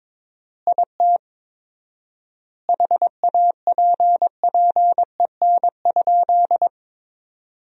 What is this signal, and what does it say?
Morse code, heard as a keyed tone, about 21 wpm, 700 Hz. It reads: IT HAPPEN?
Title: 4x Word-Spacing - Sentences from Top 400 Words - 22wpm